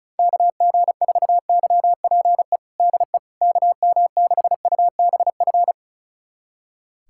Morse code, heard as a keyed tone, about 34 wpm, 700 Hz. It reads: KG4YPE DE KM6UBF
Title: USA Callsign DE Callsign 35wpm